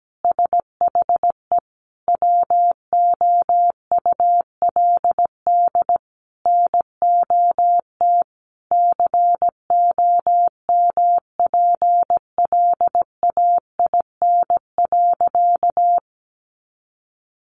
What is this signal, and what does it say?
Morse code, heard as a keyed tone, about 17 wpm, 700 Hz. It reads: SHE WOULD NOT COMPLAIN.
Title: Sentences from Top 1300 Words 17wpm